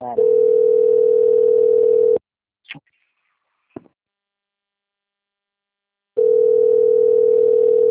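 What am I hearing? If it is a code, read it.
Ringback tone (North America), 440 + 480 Hz.
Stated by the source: Outgoing phone call recorded on the Limeira III system.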